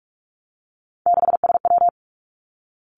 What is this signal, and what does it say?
Morse code, heard as a keyed tone, about 46 wpm, 700 Hz. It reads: 6SW
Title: Three Challenging Letter-Number Combinations 45wpm